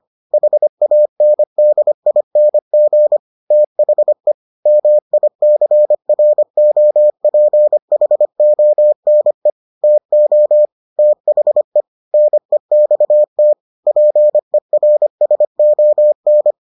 Morse code, heard as a keyed tone, about 25 wpm, 600 Hz. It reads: HANDING THE MICROPHONE TO THE NEXT PERSON